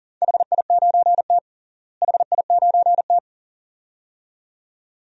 Morse code, heard as a keyed tone, about 40 wpm, 700 Hz. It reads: HI9T HI9T
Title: DXCC-100 Call Signs - 4 Characters - Repeated 2x - 40wpm